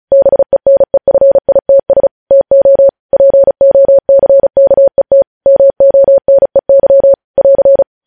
Morse code, heard as a keyed tone, about 36 wpm, 550 Hz. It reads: BENEFITS TO POCKET MONEY <AR>
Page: News headlines in Morse code at 35 wpm